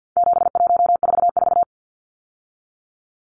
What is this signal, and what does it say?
Morse code, heard as a keyed tone, about 50 wpm, 700 Hz. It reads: 7144
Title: Numbers - 4-Digit 50wpm